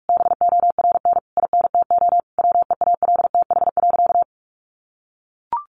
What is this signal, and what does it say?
Morse code, heard as a keyed tone, about 44 wpm, 700 Hz. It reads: BORN INTO WEALTH.
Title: Sentences from Top 1400 Words 45wpm